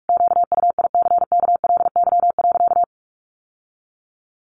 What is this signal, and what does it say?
Morse code, heard as a keyed tone, about 44 wpm, 700 Hz. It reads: QUICKLY.
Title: Sentences from Top 700 Words 45wpm